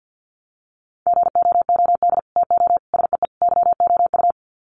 Morse code, heard as a keyed tone, about 50 wpm, 700 Hz. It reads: GOOD TO SEE YOU
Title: Sets of 4 Words 50wpm